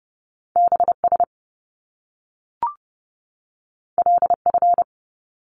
Morse code, heard as a keyed tone, about 30 wpm, 700 Hz. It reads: BS LF
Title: Top Two Letter Combinations in Words 30wpm